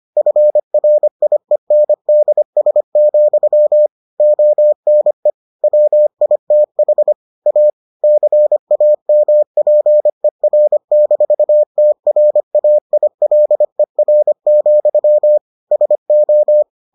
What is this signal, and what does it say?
Morse code, heard as a keyed tone, about 25 wpm, 600 Hz. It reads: FRIENDS, ONE WITH A CAMPER-TRAILER, SO